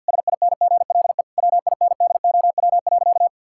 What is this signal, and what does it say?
Morse code, heard as a keyed tone, about 50 wpm, 700 Hz. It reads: SINGLE WINDOW.
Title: Sentences from Top 1600 Words 50wpm